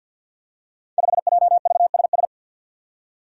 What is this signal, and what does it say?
Morse code, heard as a keyed tone, about 51 wpm, 700 Hz. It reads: 5JVSS
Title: Five Challenging Letter-Number Combinations 50wpm